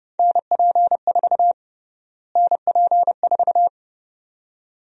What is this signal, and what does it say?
Morse code, heard as a keyed tone, about 30 wpm, 700 Hz. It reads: NP4 NP4
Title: USA Call Sign Prefixes - Repeated 2x - 30wpm